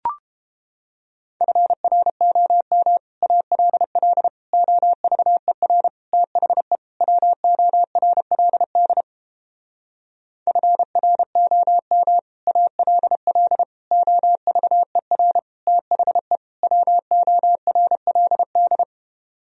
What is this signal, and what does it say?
Morse code, heard as a keyed tone, about 31 wpm, 700 Hz. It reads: FROM ALL OVER THE WORLD FROM ALL OVER THE WORLD